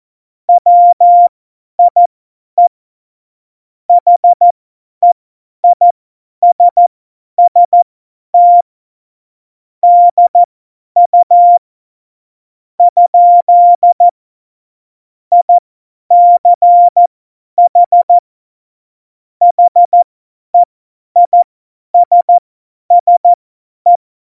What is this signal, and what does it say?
Morse code, heard as a keyed tone, about 14 wpm, spaced as for 10 wpm, 700 Hz. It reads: WIE HEISST DU? ICH HEISSE
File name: Morsekurs / Witze / Witz 21 Juli 2020, 50 F70